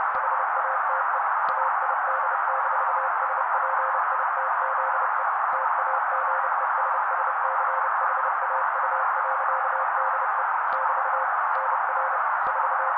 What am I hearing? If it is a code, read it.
Morse code, heard as a keyed tone, about 30 wpm, 530 Hz. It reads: HATE AID-SPITZE TAGESSCHAU.DE <BT> NR 4
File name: daily / 29wpm noise